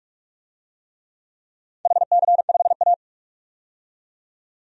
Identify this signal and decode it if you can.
Morse code, heard as a keyed tone, about 46 wpm, 700 Hz. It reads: HC5A
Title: DXCC-100 Call Signs - 4 Characters - 45wpm